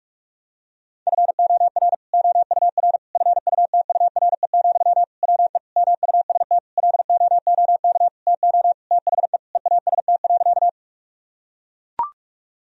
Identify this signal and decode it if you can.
Morse code, heard as a keyed tone, about 41 wpm, 700 Hz. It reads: FOR OUR FUTURE, WE MUST LOOK TO THE EAST.
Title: Sentences from Top 1000 Words - Review Part 3 - 45wpm